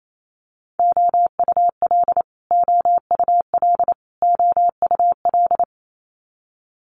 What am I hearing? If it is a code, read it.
Morse code, heard as a keyed tone, about 28 wpm, 700 Hz. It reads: OUL OUL OUL
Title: Multi-letter Phonemes Repeated 3x 28wpm